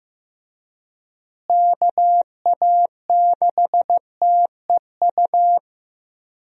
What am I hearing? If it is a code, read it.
Morse code, heard as a keyed tone, about 15 wpm, 700 Hz. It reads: KA6TEU
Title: USA Call Signs - 6 Characters - Repeated 2x - 15wpm